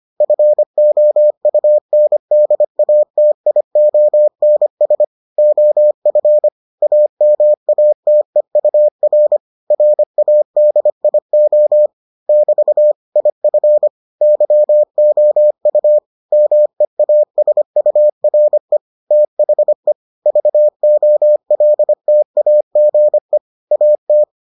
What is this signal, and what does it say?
Morse code, heard as a keyed tone, about 25 wpm, 600 Hz. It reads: FOUNDATIONS OF AMATEUR RADIO <BT> IF YOU MEASURE THE VOLTAGE AT